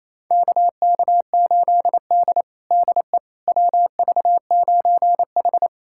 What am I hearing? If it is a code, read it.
Morse code, heard as a keyed tone, about 27 wpm, 700 Hz. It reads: KK8D DE WV9H
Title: USA Callsign DE Callsign - 4 Characters - 28wpm